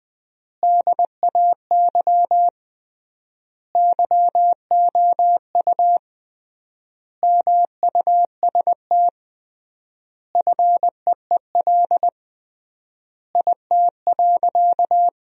Morse code, heard as a keyed tone, about 20 wpm, 700 Hz. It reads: DAY YOU MUST FEEL IT.
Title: 3x Word-Spacing - Sentences from Top 400 Words - 20wpm